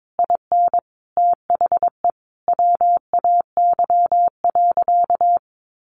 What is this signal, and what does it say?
Morse code, heard as a keyed tone, about 22 wpm, 700 Hz. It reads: IN THE WAY.
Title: Sentences from Top 1400 Words 22wpm